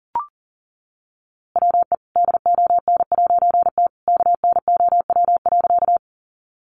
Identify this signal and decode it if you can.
Morse code, heard as a keyed tone, about 40 wpm, 700 Hz. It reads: WE DON'T KNOW.
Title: A Sound Of Thunder 40wpm